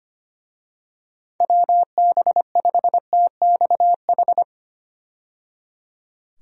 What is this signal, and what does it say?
Morse code, heard as a keyed tone, about 25 wpm, 700 Hz. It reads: WB5TXH
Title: USA Call Signs - 6 Characters - 25wpm